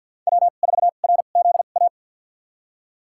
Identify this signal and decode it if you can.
Morse code, heard as a keyed tone, about 52 wpm, spaced as for 36 wpm, 700 Hz. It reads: W4RZA
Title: USA Call Signs - 5 Characters - 2x Character Spacing - 50wpm